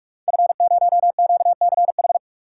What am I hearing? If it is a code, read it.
Morse code, heard as a keyed tone, about 44 wpm, 700 Hz. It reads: F0QCH
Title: Code Groups - 5 Letter:Numbers - 45wpm